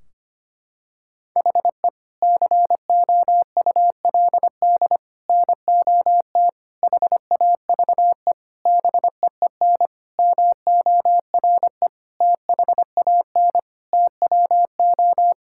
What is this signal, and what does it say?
Morse code, heard as a keyed tone, about 25 wpm, 700 Hz. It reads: HE COULD NOT HAVE BEEN MORE THAN TWO
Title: Sentences from Top 1600 Words 25wpm